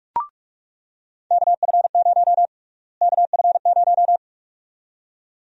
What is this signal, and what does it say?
Morse code, heard as a keyed tone, about 47 wpm, 700 Hz. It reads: KF0 KF0
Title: USA Call Sign Prefixes - Repeated 2x - 45wpm